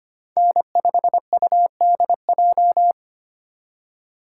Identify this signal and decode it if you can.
Morse code, heard as a keyed tone, about 25 wpm, 700 Hz. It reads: N5UDJ
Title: USA Call Signs - 5 Characters - 2x Character Spacing - 25wpm